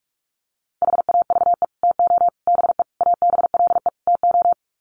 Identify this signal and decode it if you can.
Morse code, heard as a keyed tone, about 44 wpm, 700 Hz. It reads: HAVE TO BE ABLE TO
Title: Sets of 5 Words 45wpm